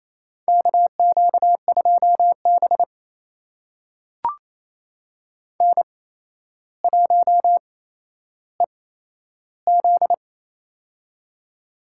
Morse code, heard as a keyed tone, about 28 wpm, spaced as for 12 wpm, 700 Hz. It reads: KQ2B N1EZ
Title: USA Call Signs - 4 Characters - 8x Character Spacing - 28wpm